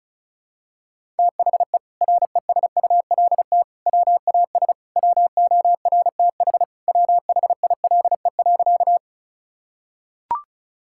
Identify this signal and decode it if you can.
Morse code, heard as a keyed tone, about 34 wpm, 700 Hz. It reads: THE RESULT WAS WORTH WHILE.
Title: Sentences from Top 1000 Words - Review Part 5 - 35wpm